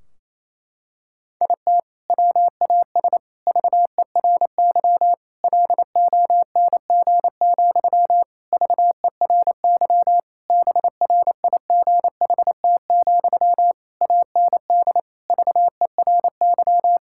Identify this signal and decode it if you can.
Morse code, heard as a keyed tone, about 28 wpm, 700 Hz. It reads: IT WAS VERY LONG, VERY BRIGHT, AND VERY
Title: Sentences from Top 1600 Words 28wpm